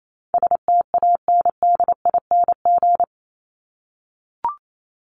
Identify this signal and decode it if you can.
Morse code, heard as a keyed tone, about 28 wpm, 700 Hz. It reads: STANDING
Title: ING Suffix - Top 100 Words with Base Words 28wpm